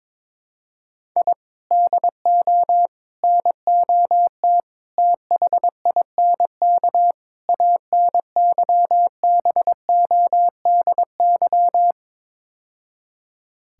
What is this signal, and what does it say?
Morse code, heard as a keyed tone, about 22 wpm, 700 Hz. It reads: I DO NOT THINK ANYBODY
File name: practice / 22wpm / Build A Phrase - 5-Word Phrases - 22wpm